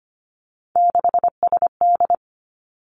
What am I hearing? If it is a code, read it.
Morse code, heard as a keyed tone, about 25 wpm, 700 Hz. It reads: 6SD
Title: Three Challenging Letter-Number Combinations 25wpm